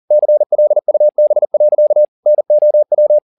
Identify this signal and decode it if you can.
Morse code, heard as a keyed tone, about 40 wpm, 600 Hz. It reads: CLUB. NOW
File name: the old man and the sea 2 40wpm